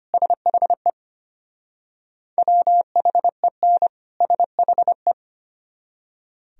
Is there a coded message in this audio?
Yes: Morse code from 0.1 to 5.1 s, heard as a keyed tone, about 33 wpm, 700 Hz. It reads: SHE WHEN SHE